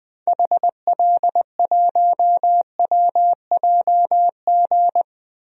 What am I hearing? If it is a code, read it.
Morse code, heard as a keyed tone, about 20 wpm, 700 Hz. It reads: HL1WJG
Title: DXCC-100 Call Signs - 6 Characters - Repeated 2x - 20wpm